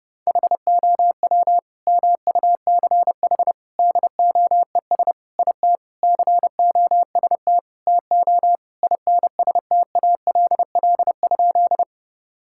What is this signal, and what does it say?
Morse code, heard as a keyed tone, about 30 wpm, 700 Hz. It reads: HOW MUCH DOES IT COST TO INSTALL?
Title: Sentences from Top 1600 Words 30wpm